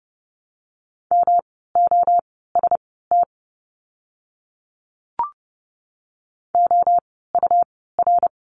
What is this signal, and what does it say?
Morse code, heard as a keyed tone, about 30 wpm, spaced as for 17 wpm, 700 Hz. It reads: MOST OUR
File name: course / 30:10wpm / 237 Lesson 38f - Letter q - Words 30wpm